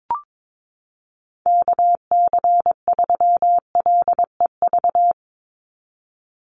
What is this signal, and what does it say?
Morse code, heard as a keyed tone, about 22 wpm, 700 Hz. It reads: KC3LEV
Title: USA Call Signs 22wpm